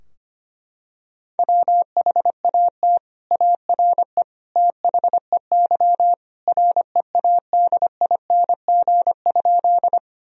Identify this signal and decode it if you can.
Morse code, heard as a keyed tone, about 25 wpm, 700 Hz. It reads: WHAT ARE THEY READING?
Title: Sentences from Top 1000 Words - Review Part 3 - 25wpm